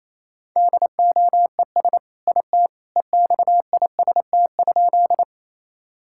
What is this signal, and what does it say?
Morse code, heard as a keyed tone, about 28 wpm, 700 Hz. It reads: DOES IT EXIST?
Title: Sentences from Top 1000 Words - Review Part 5 - 28wpm